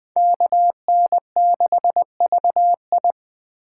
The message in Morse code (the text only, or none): KN6VI